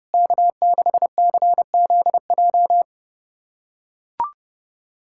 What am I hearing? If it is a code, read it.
Morse code, heard as a keyed tone, about 30 wpm, 700 Hz. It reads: K6CZJ
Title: USA Callsign DE Callsign - 5 Characters - 30wpm